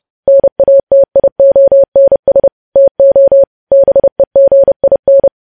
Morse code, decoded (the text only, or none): NATIONS TO BEGIN